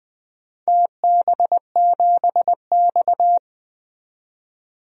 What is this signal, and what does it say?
Morse code, heard as a keyed tone, about 20 wpm, 700 Hz. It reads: TB7X
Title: DXCC-100 Call Signs - 4 Characters - 20wpm